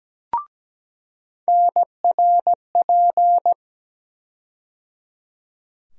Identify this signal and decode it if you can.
Morse code, heard as a keyed tone, about 17 wpm, 700 Hz. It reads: NRP
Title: USA Call Sign Postfixes - 3 Letters 17wpm